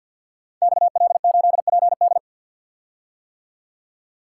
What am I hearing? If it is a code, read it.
Morse code, heard as a keyed tone, about 52 wpm, 700 Hz. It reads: XL8PD
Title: Code Groups - 5 Letter:Numbers - 50wpm